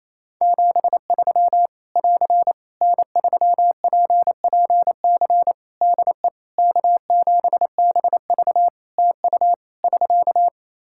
Morse code, heard as a keyed tone, about 28 wpm, 700 Hz. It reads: 73 <AR> N3PPC DE K7BV TU <SK>
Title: Practice QSO - Exchange 4 - Ending - 28wpm